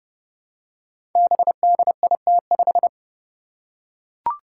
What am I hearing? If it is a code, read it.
Morse code, heard as a keyed tone, about 30 wpm, 700 Hz. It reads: BDIT5
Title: Five Challenging Letter-Number Combinations 30wpm